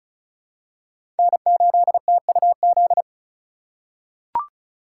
Morse code, heard as a keyed tone, about 34 wpm, 700 Hz. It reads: N8TUZ
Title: USA Call Signs - 5 Characters - 4x Character Spacing - 35wpm